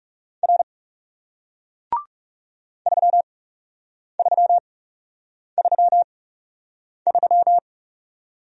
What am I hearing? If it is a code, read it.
Morse code, heard as a keyed tone, about 40 wpm, spaced as for 8 wpm, 700 Hz. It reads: R 3333